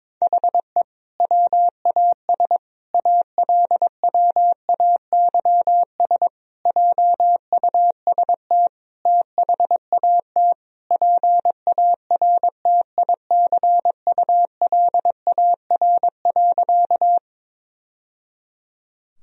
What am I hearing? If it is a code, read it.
Morse code, heard as a keyed tone, about 22 wpm, 700 Hz. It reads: HE WAS ALWAYS JUST THAT PARTICULAR.
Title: Sentences from Top 1000 Words - Review Part 4 - 22wpm